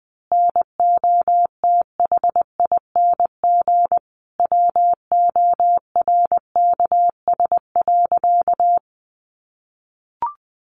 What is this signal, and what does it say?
Morse code, heard as a keyed tone, about 20 wpm, 700 Hz. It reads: NOTHING WORKS.
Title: Additional Easy Phrases 20wpm